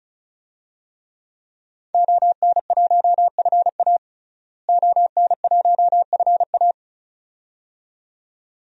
Morse code, heard as a keyed tone, about 34 wpm, 700 Hz. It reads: ON1FA ON1FA
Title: DXCC-100 Call Signs - 5 Characters - Repeated 2x - 35wpm